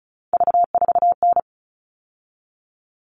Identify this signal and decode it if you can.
Morse code, heard as a keyed tone, about 34 wpm, 700 Hz. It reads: V4N
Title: All Cut Numbers - 3-Digit 35wpm